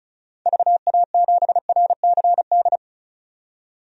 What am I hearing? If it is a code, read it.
Morse code, heard as a keyed tone, about 36 wpm, 700 Hz. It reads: VA7RCD
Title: DXCC-100 Call Signs - 6 Characters - 35wpm